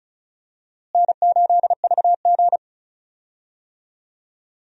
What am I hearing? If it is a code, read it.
Morse code, heard as a keyed tone, about 36 wpm, 700 Hz. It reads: N8VG
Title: USA Call Signs - 4 Characters - Repeated 2x - 35wpm